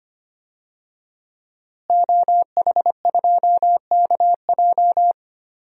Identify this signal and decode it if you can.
Morse code, heard as a keyed tone, about 25 wpm, 700 Hz. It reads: OH2KJ
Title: DXCC-100 Call Signs - 5 Characters - Repeated 2x - 25wpm